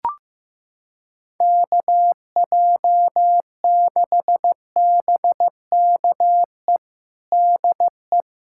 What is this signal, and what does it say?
Morse code, heard as a keyed tone, about 15 wpm, 700 Hz. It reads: KJ6BKE DE
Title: USA Callsign DE Callsign - 6 Characters - 15wpm